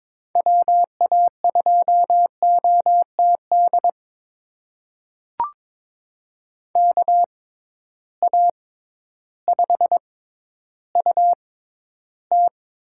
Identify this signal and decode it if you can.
Morse code, heard as a keyed tone, about 22 wpm, spaced as for 12 wpm, 700 Hz. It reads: WA2OTD KA5UT
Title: USA Call Signs - 6 Characters - 6x Character Spacing - 22wpm